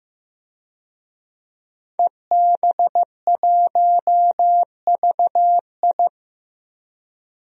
Morse code, heard as a keyed tone, about 15 wpm, 700 Hz. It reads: EB1VI
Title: DXCC-100 Call Signs - 5 Characters - 15wpm